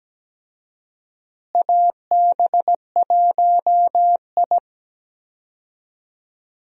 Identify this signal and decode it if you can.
Morse code, heard as a keyed tone, about 17 wpm, 700 Hz. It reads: AB1I
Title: USA Call Signs - 4 Characters - Repeated 3x - 17wpm